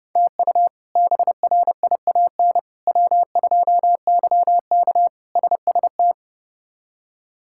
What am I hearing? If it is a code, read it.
Morse code, heard as a keyed tone, about 30 wpm, 700 Hz. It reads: TU BRIAN W2YK SST
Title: K1USN SST QSO Practice - 30wpm